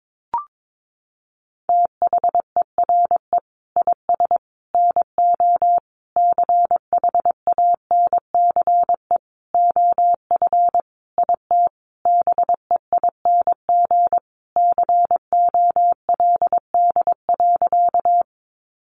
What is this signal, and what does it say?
Morse code, heard as a keyed tone, about 22 wpm, 700 Hz. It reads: THERE IS NO CHANCE OF IT BEING COLD.
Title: Sentences from Top 800 Words 22wpm